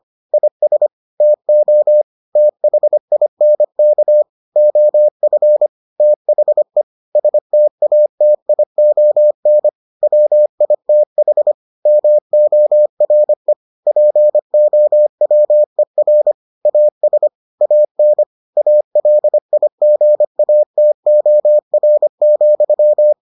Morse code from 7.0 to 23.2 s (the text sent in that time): STATION WITH MORE POWER AS AN ALIGATOR,